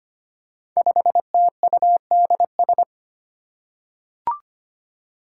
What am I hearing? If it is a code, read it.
Morse code, heard as a keyed tone, about 25 wpm, 700 Hz. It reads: 5TUDS
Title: Five Challenging Letter-Number Combinations 25wpm